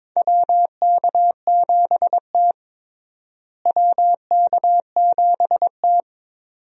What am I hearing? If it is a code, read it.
Morse code, heard as a keyed tone, about 22 wpm, 700 Hz. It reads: WK7T WK7T
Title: USA Call Signs - 4 Characters - Repeated 2x - 22wpm